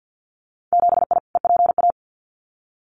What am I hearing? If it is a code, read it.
Morse code, heard as a keyed tone, about 50 wpm, 700 Hz. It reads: 7I EPA
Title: Winter Field Day Exchanges Encore 50wpm